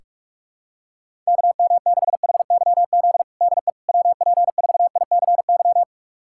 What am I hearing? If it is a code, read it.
Morse code, heard as a keyed tone, about 46 wpm, 700 Hz. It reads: KM6HYZ DE WP4ICY
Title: USA Callsign DE Callsign - 6 Characters - 45wpm